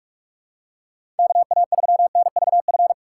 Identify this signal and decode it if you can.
Morse code, heard as a keyed tone, about 46 wpm, 700 Hz. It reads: KA3NVF
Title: USA Call Signs - 6 Characters - 6x Character Spacing - 45wpm